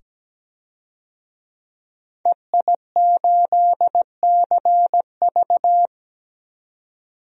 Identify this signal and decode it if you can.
Morse code, heard as a keyed tone, about 17 wpm, 700 Hz. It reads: EI8CV
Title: DXCC-100 Call Signs - 5 Characters - 17wpm